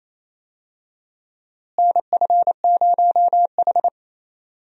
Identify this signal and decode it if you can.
Morse code, heard as a keyed tone, about 27 wpm, 700 Hz. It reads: NF0H
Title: USA Call Signs - 4 Characters - Repeated 2x - 28wpm